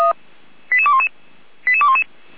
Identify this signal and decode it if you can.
Five-tone selective call (ZVEI), heard as quick runs of five tones, tones of about 70 ms. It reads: 88211 88211
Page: /455/SDR/ZVEI/88235/